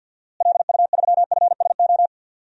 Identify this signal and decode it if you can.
Morse code, heard as a keyed tone, about 52 wpm, 700 Hz. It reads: LU3FSO
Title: DXCC-100 Call Signs - 6 Characters - Repeated 2x - 50wpm